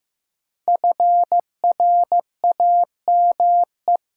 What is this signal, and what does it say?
Morse code, heard as a keyed tone, about 15 wpm, 700 Hz. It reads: FRAME
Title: Sentences from Top 1000 Words - Review Part 6 - New Word Drill - 15wpm